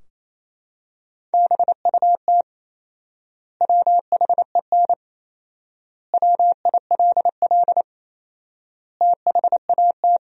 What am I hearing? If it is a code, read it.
Morse code, heard as a keyed tone, about 27 wpm, 700 Hz. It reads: BUT WHEN WILL THAT
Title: 4x Word-Spacing - Sentences from Top 100 Words - 28wpm